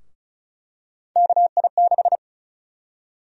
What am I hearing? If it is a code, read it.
Morse code, heard as a keyed tone, about 36 wpm, 700 Hz. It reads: KI6
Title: USA Call Sign Prefixes - 35wpm